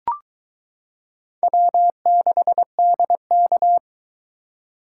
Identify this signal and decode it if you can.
Morse code, heard as a keyed tone, about 22 wpm, 700 Hz. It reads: W6DK